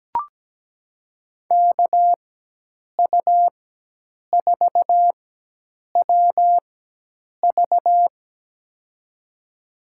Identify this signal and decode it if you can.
Morse code, heard as a keyed tone, about 17 wpm, spaced as for 8 wpm, 700 Hz. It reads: KU4WV
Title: USA Call Signs - 5 Characters - 4x Character Spacing - 17wpm